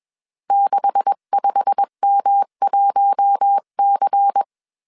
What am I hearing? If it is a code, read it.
Morse code, heard as a keyed tone, about 21 wpm, 800 Hz. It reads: 65M1C